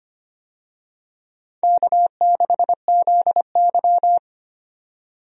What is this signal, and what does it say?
Morse code, heard as a keyed tone, about 25 wpm, 700 Hz. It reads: K6ZY
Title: USA Call Signs - 4 Characters - 25wpm